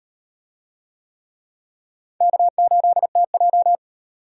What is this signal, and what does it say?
Morse code, heard as a keyed tone, about 38 wpm, 700 Hz. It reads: K8TJ